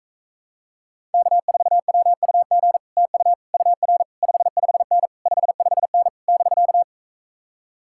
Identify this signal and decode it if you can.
Morse code, heard as a keyed tone, about 44 wpm, 700 Hz. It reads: K4WUG TU UR 55N 55N <BK>